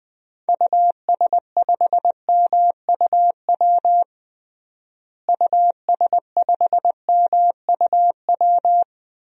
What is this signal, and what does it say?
Morse code, heard as a keyed tone, about 19 wpm, 700 Hz. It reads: US5MUW US5MUW
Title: DXCC-100 Call Signs - 6 Characters - Repeated 3x - 20wpm